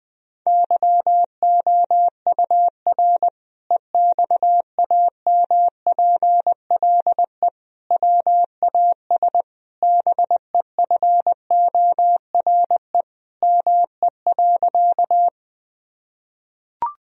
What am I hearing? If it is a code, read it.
Morse code, heard as a keyed tone, about 20 wpm, 700 Hz. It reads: YOUR EXAMPLE WAS BEFORE ME.
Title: Sentences from Top 1000 Words - Review Part 1 - 20wpm